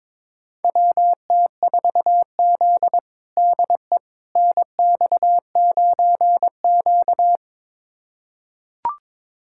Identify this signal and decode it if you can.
Morse code, heard as a keyed tone, about 22 wpm, 700 Hz. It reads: WT4Z DE NX9Q
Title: USA Callsign DE Callsign - 4 Characters - 22wpm